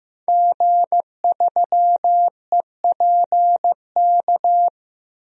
Morse code, heard as a keyed tone, about 15 wpm, 700 Hz. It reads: G3EPK
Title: DXCC-100 Call Signs - 5 Characters - 15wpm